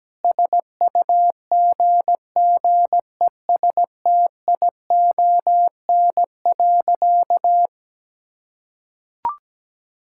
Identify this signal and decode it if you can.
Morse code, heard as a keyed tone, about 17 wpm, 700 Hz. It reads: SUGGESTION.
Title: Sentences from Top 1600 Words 17wpm